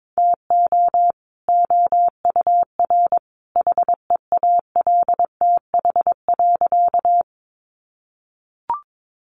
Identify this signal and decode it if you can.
Morse code, heard as a keyed tone, about 22 wpm, 700 Hz. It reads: TO OUR HEALTH.
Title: Sentences from Top 600 Words 22wpm